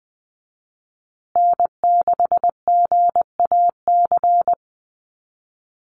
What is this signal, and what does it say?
Morse code, heard as a keyed tone, about 20 wpm, 700 Hz. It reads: N6GAC
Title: USA Call Signs Repeated 3x 20wpm